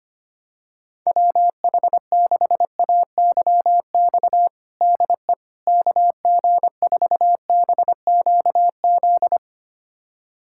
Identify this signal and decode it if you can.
Morse code, heard as a keyed tone, about 25 wpm, 700 Hz. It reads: WH6AYX DE KG4BQZ
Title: USA Callsign DE Callsign 25wpm